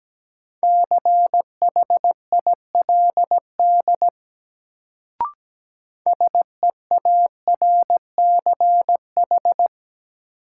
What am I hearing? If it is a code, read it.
Morse code, heard as a keyed tone, about 17 wpm, 700 Hz. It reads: CHILD SEARCH
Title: Sentences from Top 800 Words - New Word Drill - 17wpm